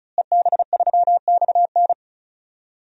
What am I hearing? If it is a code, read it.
Morse code, heard as a keyed tone, about 34 wpm, 700 Hz. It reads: EB3XN
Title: DXCC-100 Call Signs - 5 Characters - Repeated 3x - 35wpm